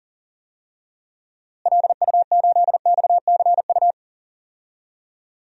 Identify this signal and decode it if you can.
Morse code, heard as a keyed tone, about 39 wpm, 700 Hz. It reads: LU8XCU
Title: DXCC-100 Call Signs - 6 Characters - Repeated 2x - 40wpm